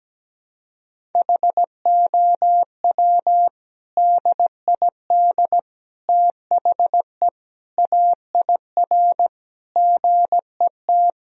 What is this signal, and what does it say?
Morse code, heard as a keyed tone, about 17 wpm, 700 Hz. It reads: HOW DID THE AIR GET
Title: Sentences from Top 1000 Words - Review Part 1 - 17wpm